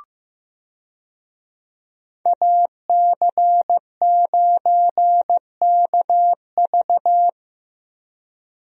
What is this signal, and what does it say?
Morse code, heard as a keyed tone, about 15 wpm, 700 Hz. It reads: AC9KV
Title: USA Call Signs - 5 Characters - Repeated 2x - 15wpm